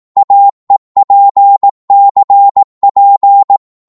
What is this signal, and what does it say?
Morse code, heard as a keyed tone, about 18 wpm, 800 Hz. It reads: AEPCP
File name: Morse / ltr-18wpm